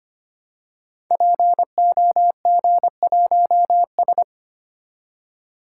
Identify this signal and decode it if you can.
Morse code, heard as a keyed tone, about 25 wpm, 700 Hz. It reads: POG1S
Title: Code Groups - 5 Letter:Numbers - 25wpm